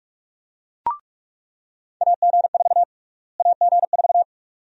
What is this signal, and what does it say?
Morse code, heard as a keyed tone, about 46 wpm, 700 Hz. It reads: AG4 AG4
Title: USA Call Sign Prefixes - Repeated 2x - 45wpm